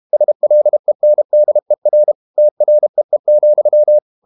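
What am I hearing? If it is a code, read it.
Morse code, heard as a keyed tone, about 32 wpm, 600 Hz. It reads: SLENDER TREE,
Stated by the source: Morse code practice at 32 wpm from CW med Gnister, reading Ken Follett's novel Never in English.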